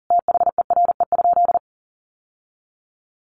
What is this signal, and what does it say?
Morse code, heard as a keyed tone, about 40 wpm, 700 Hz. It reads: THERE?
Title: Sentences from Top 600 Words 40wpm